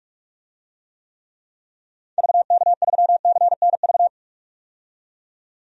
Morse code, heard as a keyed tone, about 44 wpm, 700 Hz. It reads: VK3CNV
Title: DXCC-100 Call Signs - 6 Characters - 45wpm